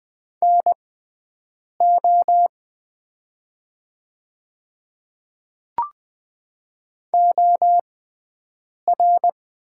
Morse code, heard as a keyed tone, about 20 wpm, spaced as for 7 wpm, 700 Hz. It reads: NO OR
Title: 6x Character Spacing - Top 200 Words - 20wpm